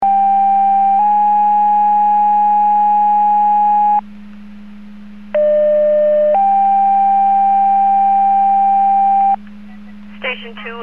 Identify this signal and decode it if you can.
Two-tone page, heard as a one-second tone, then a three-second tone, about 790 then 830 Hz, about 600 then 790 Hz, with speech. It is heard from 0.0 to 9.4 s.